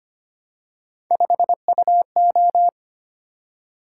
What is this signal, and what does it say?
Morse code, heard as a keyed tone, about 25 wpm, 700 Hz. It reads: 5UO
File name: Three Letter-Number Combinations - 25wpm